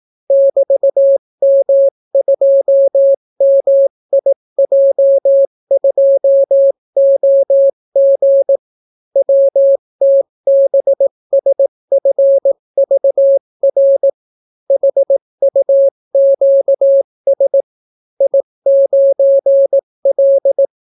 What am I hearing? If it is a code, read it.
Morse code, heard as a keyed tone, about 18 wpm, 550 Hz. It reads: <BT>M2MIJ2OG WTBSFVR HUQS I9L